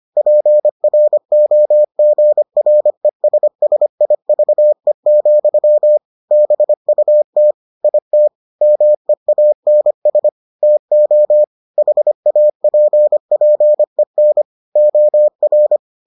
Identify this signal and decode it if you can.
Morse code, heard as a keyed tone, about 25 wpm, 600 Hz. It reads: PROGRESSIVE, BUT IT MEANS TO HAPPEN OR